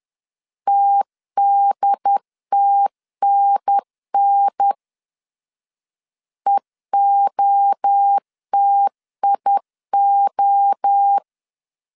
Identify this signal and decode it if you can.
Morse code, heard as a keyed tone, about 10 wpm, 800 Hz. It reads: TDTNN EOTIO
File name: CW / DET / 1 3 1500C S B 10wpm 3335